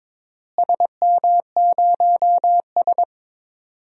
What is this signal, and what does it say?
Morse code, heard as a keyed tone, about 22 wpm, 700 Hz. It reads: SM0S
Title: DXCC-100 Call Signs - 4 Characters - Repeated 2x - 22wpm